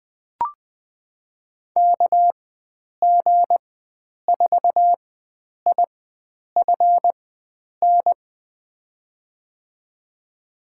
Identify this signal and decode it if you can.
Morse code, heard as a keyed tone, about 20 wpm, spaced as for 9 wpm, 700 Hz. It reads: KG4IFN
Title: USA Call Signs - 6 Characters - 4x Character Spacing - 20wpm